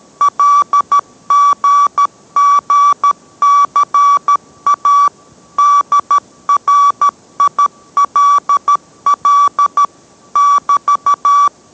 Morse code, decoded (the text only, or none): LGGCA DRILL <BT>